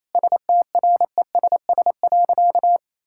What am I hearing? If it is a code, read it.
Morse code, heard as a keyed tone, about 27 wpm, 700 Hz. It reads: STRESS.
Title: Sentences from Top 1000 Words - Review Part 6 - 28wpm